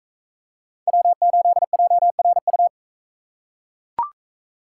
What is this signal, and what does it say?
Morse code, heard as a keyed tone, about 40 wpm, 700 Hz. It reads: W8JRU